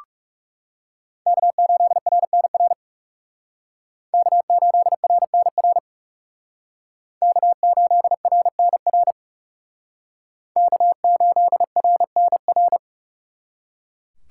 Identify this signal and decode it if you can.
Morse code, heard as a keyed tone, about 36 wpm, 700 Hz. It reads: K8RNR K8RNR K8RNR K8RNR
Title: Speed Racing - USA 1x3 Call Signs 30wpm